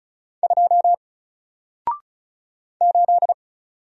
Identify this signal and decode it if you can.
Morse code, heard as a keyed tone, about 36 wpm, 700 Hz. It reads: 2 8
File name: Single-Letter-Number-35wpm